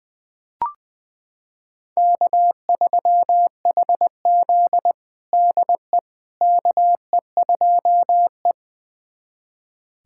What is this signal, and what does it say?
Morse code, heard as a keyed tone, about 21 wpm, 700 Hz. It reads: K3HZ DE KE2E
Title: USA Callsign DE Callsign - 4 Characters - 20wpm